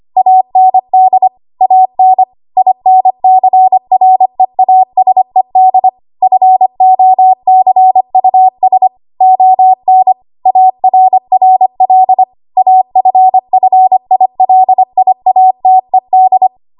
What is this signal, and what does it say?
Morse code, heard as a keyed tone, about 25 wpm, 750 Hz. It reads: AND AN INCREASED FOCUS ON ARRL AFFILIATED
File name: Archive / 25 WPM / 25WPM